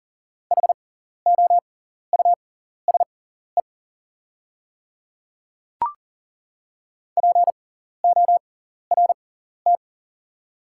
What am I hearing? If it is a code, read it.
Morse code, heard as a keyed tone, about 40 wpm, spaced as for 14 wpm, 700 Hz. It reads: HOUSE PORT
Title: 6x Character Spacing - Top 300 Words - 40wpm